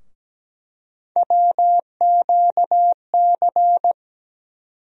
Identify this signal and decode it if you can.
Morse code, heard as a keyed tone, about 17 wpm, 700 Hz. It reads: WQC
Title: USA Call Sign Postfixes - 3 Letters 17wpm